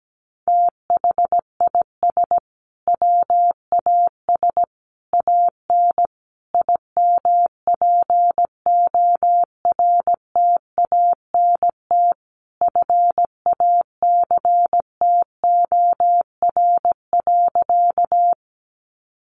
Morse code, heard as a keyed tone, about 17 wpm, 700 Hz. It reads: THIS WAS AN IMPORTANT FACTOR.
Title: Sentences from Top 1300 Words 17wpm